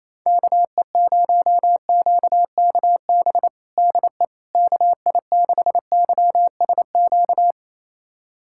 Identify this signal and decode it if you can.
Morse code, heard as a keyed tone, about 27 wpm, 700 Hz. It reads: KE0QKB DE KI6YSQ
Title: USA Callsign DE Callsign - 6 Characters - 28wpm